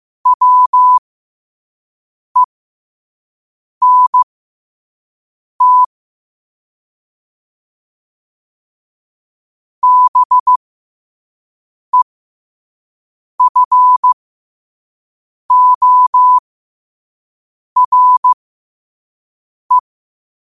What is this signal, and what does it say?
Morse code, heard as a keyed tone, about 15 wpm, spaced as for 5 wpm, 1000 Hz. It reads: WENT BEFORE
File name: qotdfiles / 5wpm